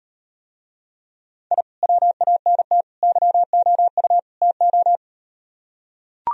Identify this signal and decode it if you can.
Morse code, heard as a keyed tone, about 38 wpm, 700 Hz. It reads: I WANT YOU TO